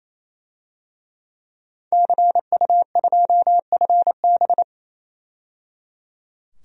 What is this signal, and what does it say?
Morse code, heard as a keyed tone, about 28 wpm, 700 Hz. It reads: CU2FB
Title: DXCC-100 Call Signs - 5 Characters - 28wpm